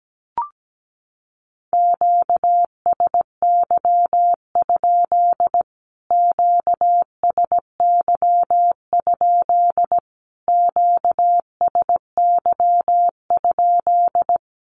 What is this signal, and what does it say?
Morse code, heard as a keyed tone, about 17 wpm, 700 Hz. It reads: QSY? QSY? QSY?
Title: Q-Codes Repeated 3x 17wpm